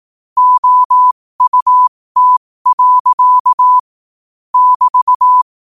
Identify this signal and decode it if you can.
Morse code, heard as a keyed tone, about 18 wpm, spaced as for 15 wpm, 1000 Hz. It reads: OUT. <BT>
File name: qotdfiles / 15wpm